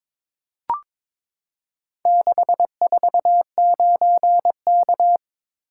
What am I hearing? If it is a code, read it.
Morse code, heard as a keyed tone, about 22 wpm, 700 Hz. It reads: 649K
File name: practice / 22wpm / Four Letter-Number Combinations - 22wpm